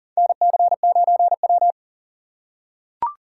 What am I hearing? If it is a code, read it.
Morse code, heard as a keyed tone, about 40 wpm, 700 Hz. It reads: NC9W
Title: K1USN SST QSO Practice - 40wpm